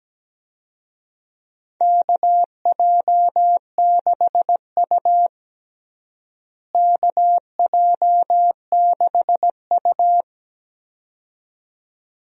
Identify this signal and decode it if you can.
Morse code, heard as a keyed tone, about 17 wpm, 700 Hz. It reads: KJ6U KJ6U
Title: USA Call Signs - 4 Characters - Repeated 2x - 17wpm